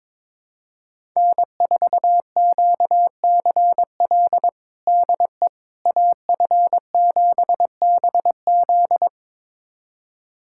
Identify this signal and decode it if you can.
Morse code, heard as a keyed tone, about 22 wpm, 700 Hz. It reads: N4QCL DE AF7BZ
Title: USA Callsign DE Callsign - 5 Characters - 22wpm